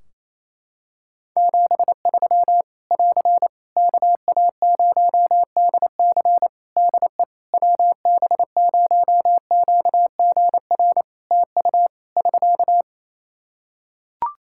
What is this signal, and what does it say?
Morse code, heard as a keyed tone, about 27 wpm, 700 Hz. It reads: 73 <AR> KA0DC DE WB0QGR TU <SK>
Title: Practice QSO - Exchange 4 - Ending - 28wpm